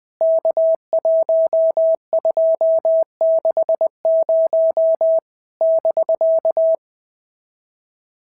Morse code, heard as a keyed tone, about 20 wpm, 650 Hz. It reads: K1260 <BK>
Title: POTA QSO Practice - 20wpm